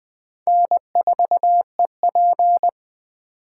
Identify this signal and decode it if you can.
Morse code, heard as a keyed tone, about 20 wpm, 700 Hz. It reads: N4EP
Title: Speed Racing - USA 1x2 Call Signs 17wpm